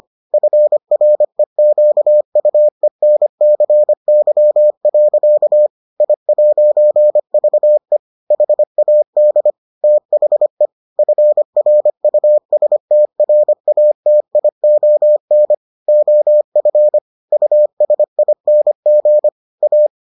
Morse code, heard as a keyed tone, about 25 wpm, 600 Hz. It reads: FREQUENCY. I'VE HAD THE FRUSTRATION OF USING A